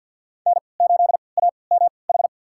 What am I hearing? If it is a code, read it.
Morse code, heard as a keyed tone, about 52 wpm, spaced as for 29 wpm, 700 Hz. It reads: N8AMH
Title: USA Call Signs - 3x Character Spacing - 50wpm